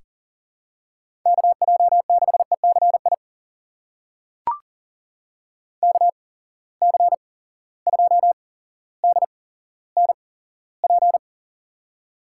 Morse code, heard as a keyed tone, about 40 wpm, spaced as for 17 wpm, 700 Hz. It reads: KJ6ECI KC2DNP
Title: USA Call Signs - 6 Characters - 8x Character Spacing - 40wpm